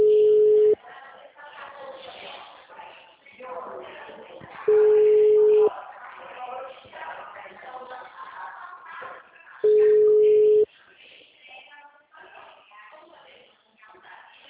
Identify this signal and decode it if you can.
Ringback tone (Europe, 425 Hz), with speech.